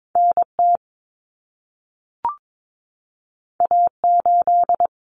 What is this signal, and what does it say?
Morse code, heard as a keyed tone, about 22 wpm, 700 Hz. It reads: NT A8
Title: Common Cut Numbers - 2-Digit 22wpm